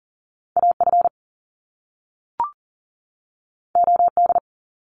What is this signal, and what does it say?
Morse code, heard as a keyed tone, about 38 wpm, 700 Hz. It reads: AF OD
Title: Top Two Letter Combinations in Words 40wpm